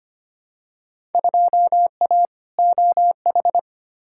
Morse code, heard as a keyed tone, about 25 wpm, 700 Hz. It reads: 2A OH
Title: Field Day Exchanges - 25wpm